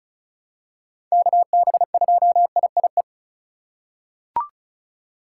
Morse code, heard as a keyed tone, about 36 wpm, 700 Hz. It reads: KB2IIE
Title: USA Call Signs - 6 Characters - 6x Character Spacing - 35wpm